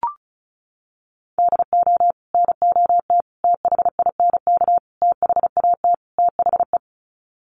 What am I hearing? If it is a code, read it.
Morse code, heard as a keyed tone, about 34 wpm, 700 Hz. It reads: DO NOT THINK THAT THE
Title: Sets of 5 Words Encore2 35wpm